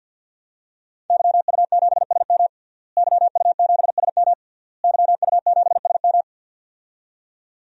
Morse code, heard as a keyed tone, about 49 wpm, 700 Hz. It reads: YU7SM YU7SM YU7SM
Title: DXCC-100 Call Signs - 5 Characters - Repeated 3x - 50wpm